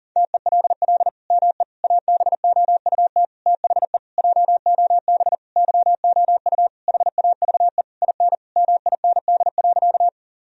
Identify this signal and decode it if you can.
Morse code, heard as a keyed tone, about 37 wpm, 700 Hz. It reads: TELL ME ABOUT THE JOB YOU HAVE IN MIND.
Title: Sentences from Top 1000 Words - Review Part 3 - 40wpm